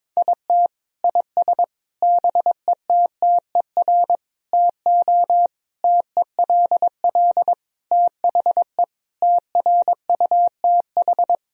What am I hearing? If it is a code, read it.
Morse code, heard as a keyed tone, about 22 wpm, 700 Hz. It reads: IT IS BETTER TO TELL THE TRUTH
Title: Sentences from Top 1000 Words - Review Part 4 - 22wpm